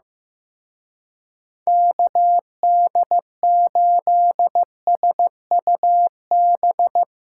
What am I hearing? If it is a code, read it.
Morse code, heard as a keyed tone, about 18 wpm, 700 Hz. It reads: KD8SUB
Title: Speed Racing - USA Call Signs 15wpm